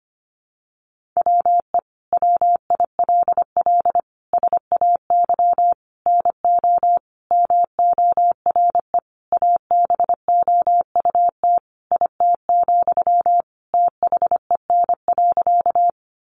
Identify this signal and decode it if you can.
Morse code, heard as a keyed tone, about 25 wpm, 700 Hz. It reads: WE WILL SAY NO MORE ABOUT IT, THEN.
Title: Sentences from Top 200 Words 25wpm